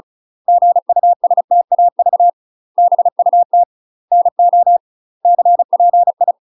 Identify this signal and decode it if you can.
Morse code, heard as a keyed tone, about 36 wpm, 700 Hz. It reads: GUSTAV BUT NO CPI
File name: UK MASTER QSOs TRAINING / 35WPM / UK MASTER QSOs 0020